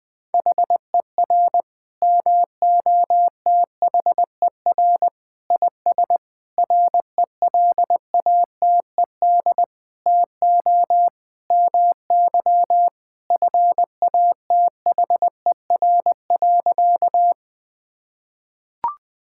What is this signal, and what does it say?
Morse code, heard as a keyed tone, about 21 wpm, 700 Hz. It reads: HER MOTHER IS RELATED TO MY FATHER.
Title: Sentences from Top 1000 Words - Review Part 5 - 20wpm